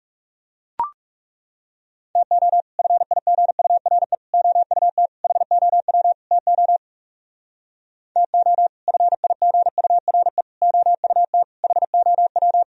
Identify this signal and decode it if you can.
Morse code, heard as a keyed tone, about 36 wpm, 700 Hz. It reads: TO FIGURE OUT HOW TO TO FIGURE OUT HOW
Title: Speed-Racing - Sets of 5 Words 30wpm